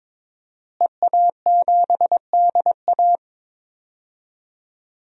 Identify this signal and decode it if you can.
Morse code, heard as a keyed tone, about 22 wpm, 700 Hz. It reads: EA7DA
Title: DXCC-100 Call Signs - 5 Characters - Repeated 3x - 22wpm